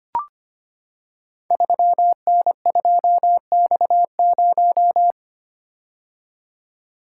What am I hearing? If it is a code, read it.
Morse code, heard as a keyed tone, about 25 wpm, 700 Hz. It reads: 3N2X0
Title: Code Groups - 5 Letter:Numbers - 25wpm